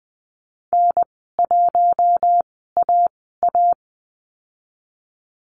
Morse code, heard as a keyed tone, about 20 wpm, spaced as for 15 wpm, 700 Hz. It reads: N1AA
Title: 186 Lesson 30g - Letter b - Call Signs 20wpm